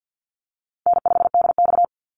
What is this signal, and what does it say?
Morse code, heard as a keyed tone, about 50 wpm, 700 Hz. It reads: N5DX
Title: Easy Call Signs Repeated 3x - 50wpm